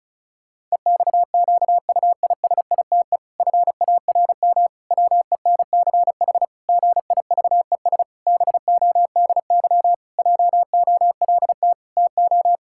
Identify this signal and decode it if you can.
Morse code, heard as a keyed tone, about 34 wpm, 700 Hz. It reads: EXQUISITE FARM WENCH GIVES BODY JOLT TO
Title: Pangram Challenge v2 - 35wpm